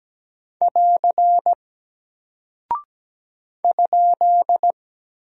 Morse code, heard as a keyed tone, about 17 wpm, 700 Hz. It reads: <AR> ?
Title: Single Character (Letter:Number:Punctuation:Prosign) Alt 17wpm